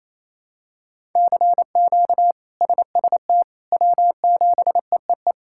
Morse code, heard as a keyed tone, about 28 wpm, 700 Hz. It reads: CQ SST W7EEE
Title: K1USN SST QSO Practice - 28wpm